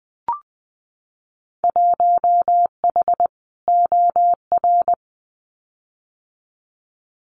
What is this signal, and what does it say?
Morse code, heard as a keyed tone, about 20 wpm, 700 Hz. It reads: 1H OR
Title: Winter Field Day Exchanges 20wpm